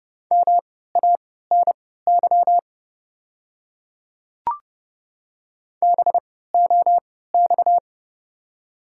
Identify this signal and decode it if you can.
Morse code, heard as a keyed tone, about 31 wpm, spaced as for 17 wpm, 700 Hz. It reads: MANY BOX
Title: 3x Character Spacing - Top 500 Words - 30wpm